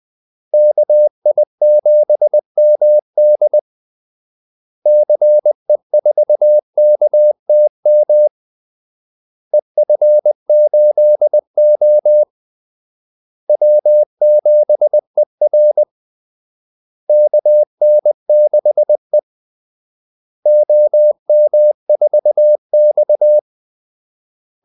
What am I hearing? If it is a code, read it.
Morse code, heard as a keyed tone, about 20 wpm, 600 Hz. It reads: KI7MD CE4KTM EF8O W7ER KN6E OM4X
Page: Kaldesignaler 20 wpm